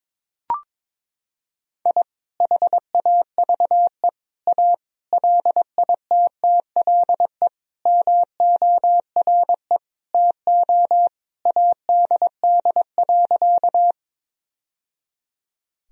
Morse code, heard as a keyed tone, about 22 wpm, 700 Hz. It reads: I HAVE A LITTLE MORE TO ADD.
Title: Sentences from Top 1000 Words - Review Part 1 - 22wpm